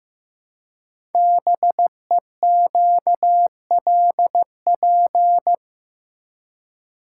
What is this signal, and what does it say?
Morse code, heard as a keyed tone, about 15 wpm, 700 Hz. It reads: BEQLP
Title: Code Groups - 5 Letters - 15wpm